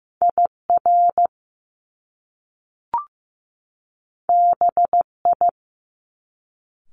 Morse code, heard as a keyed tone, about 15 wpm, 700 Hz. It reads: IR BI
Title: Top Two Letter Combinations in Words 15wpm